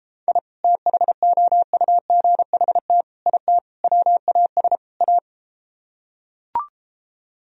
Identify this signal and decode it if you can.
Morse code, heard as a keyed tone, about 33 wpm, 700 Hz. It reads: I THOUGHT IT WAS A